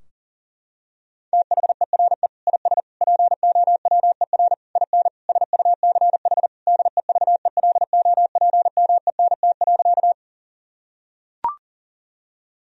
Morse code, heard as a keyed tone, about 40 wpm, 700 Hz. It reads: THERE IS POWER IN SUCH DEVELOPMENT.
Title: Sentences from Top 1000 Words - Review Part 4 - 40wpm